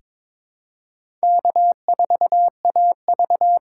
Morse code, heard as a keyed tone, about 22 wpm, 700 Hz. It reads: K4AV
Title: USA Call Signs - 6x Character Spacing - 22wpm